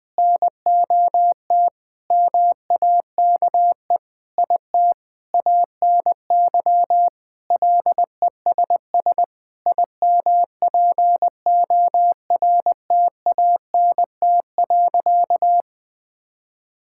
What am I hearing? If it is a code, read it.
Morse code, heard as a keyed tone, about 20 wpm, 700 Hz. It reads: NOT MAKE IT ANY LESS IMPORTANT.
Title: Sentences with 7-Word Phrases - 20wpm